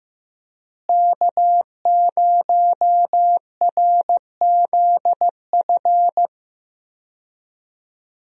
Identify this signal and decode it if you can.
Morse code, heard as a keyed tone, about 15 wpm, 700 Hz. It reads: K0RZF
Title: USA Call Signs - 5 Characters - Repeated 2x - 15wpm